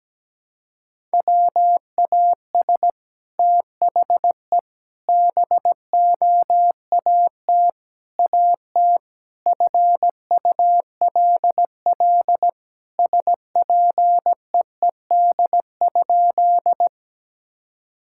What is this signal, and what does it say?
Morse code, heard as a keyed tone, about 17 wpm, 700 Hz. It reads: WAS THE BOAT AT FULL SPEED?
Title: Sentences from Top 1000 Words - Review Part 5 - 17wpm